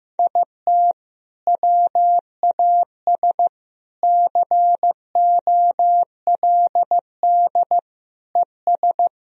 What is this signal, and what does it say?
Morse code, heard as a keyed tone, about 15 wpm, 700 Hz. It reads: IT WAS COLD ES
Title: Sentences from Top 500 Words and Abbreviations 15wpm